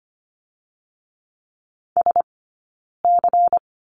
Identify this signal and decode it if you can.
Morse code, heard as a keyed tone, about 25 wpm, 700 Hz. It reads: S C
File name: course / 25:10wpm / 092 Lesson 16d - Letter c - Two Characters 25wpm